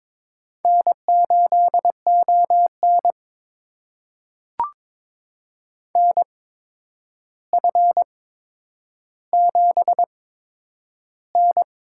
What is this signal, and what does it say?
Morse code, heard as a keyed tone, about 22 wpm, spaced as for 9 wpm, 700 Hz. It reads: N8ON NF7N
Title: USA Call Signs - 4 Characters - 8x Character Spacing - 22wpm